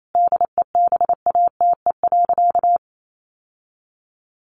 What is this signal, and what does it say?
Morse code, heard as a keyed tone, about 28 wpm, 700 Hz. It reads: DEBATE.
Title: Sentences from Top 1100 Words 28wpm